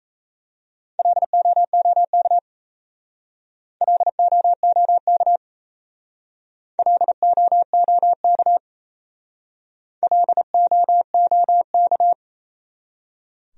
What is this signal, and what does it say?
Morse code, heard as a keyed tone, about 37 wpm, 700 Hz. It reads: LOOK LOOK LOOK LOOK